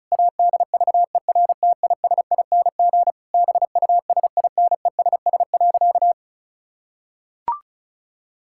Morse code, heard as a keyed tone, about 34 wpm, 700 Hz. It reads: ADVERTISING BUSINESS.
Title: Sentences from Top 1000 Words - Review Part 6 - 35wpm